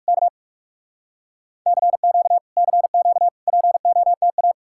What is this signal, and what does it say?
Morse code, heard as a keyed tone, about 59 wpm, spaced as for 52 wpm, 700 Hz. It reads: K CQ CQ POTA
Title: Speed-Racing Common QSO Phrases - 45wpm